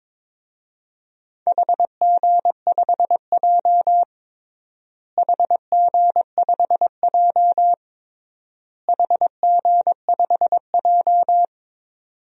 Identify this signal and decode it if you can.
Morse code, heard as a keyed tone, about 22 wpm, 700 Hz. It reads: HG5J HG5J HG5J
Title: DXCC-100 Call Signs - 4 Characters - Repeated 3x - 22wpm